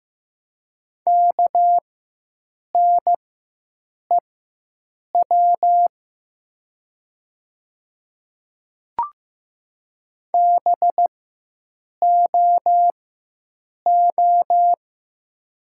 Morse code, heard as a keyed tone, about 15 wpm, spaced as for 7 wpm, 700 Hz. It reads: KNEW BOO
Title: 4x Character Spacing - Top 500 Words - 15wpm